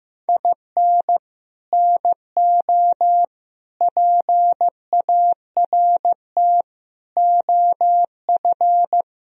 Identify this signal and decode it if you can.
Morse code, heard as a keyed tone, about 15 wpm, 700 Hz. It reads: IN NO PART OF
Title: Sentences from Top 1000 Words - Review Part 3 - 15wpm